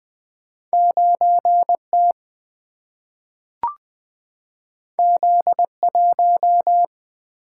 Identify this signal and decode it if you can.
Morse code, heard as a keyed tone, about 20 wpm, 700 Hz. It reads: 9T Z1
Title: Single-Letter Single-Number 20wpm